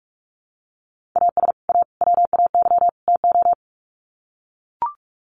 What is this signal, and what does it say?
Morse code, heard as a keyed tone, about 44 wpm, 700 Hz. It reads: AS A WAY TO
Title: Sets of 4 Words 45wpm